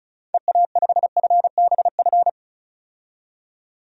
Morse code, heard as a keyed tone, about 34 wpm, 700 Hz. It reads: EA5FBF
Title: DXCC-100 Call Signs - 6 Characters - 35wpm